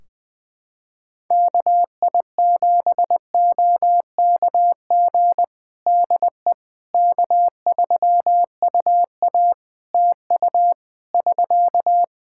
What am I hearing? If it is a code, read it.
Morse code, heard as a keyed tone, about 20 wpm, 700 Hz. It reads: KI7OKG DE K3UA TU <SK>
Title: Practice QSO - Exchange 4 - Ending - 20wpm